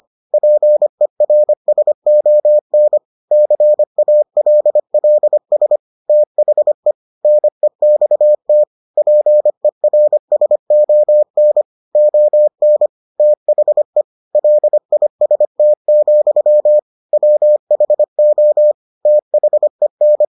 Morse code, heard as a keyed tone, about 25 wpm, 600 Hz. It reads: PERSON CALLS THE NEXT PERSON ON THE LIST, WHO THEN